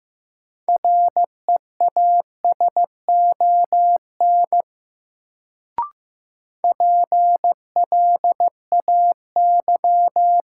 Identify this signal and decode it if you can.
Morse code, heard as a keyed tone, about 15 wpm, 700 Hz. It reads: REASON PLAY
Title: Sets of 5 Words Encore - New Word Drill - 15wpm